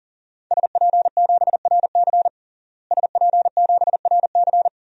Morse code, heard as a keyed tone, about 37 wpm, 700 Hz. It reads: SP7RC SP7RC
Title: DXCC-100 Call Signs - 5 Characters - Repeated 2x - 40wpm